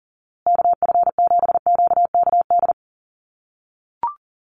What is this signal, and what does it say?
Morse code, heard as a keyed tone, about 40 wpm, 700 Hz. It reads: KF7QKD
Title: USA Call Signs Repeated 2x 40wpm